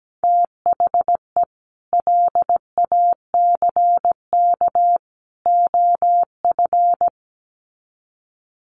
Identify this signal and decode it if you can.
Morse code, heard as a keyed tone, about 17 wpm, 700 Hz. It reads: THE LACK OF
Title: Sets of 3 Words 17wpm